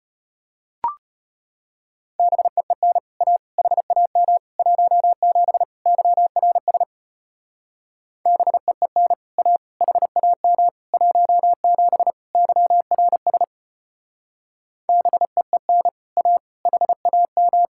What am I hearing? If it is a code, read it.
Morse code, heard as a keyed tone, about 33 wpm, 700 Hz. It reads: BEEN A HAM 17 YRS BEEN A HAM 17 YRS BEEN A HAM